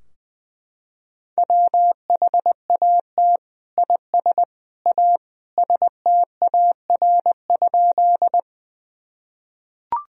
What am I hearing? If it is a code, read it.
Morse code, heard as a keyed tone, about 21 wpm, 700 Hz. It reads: WHAT IS A STAR?
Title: Sentences from Top 1000 Words - Review Part 2 - 20wpm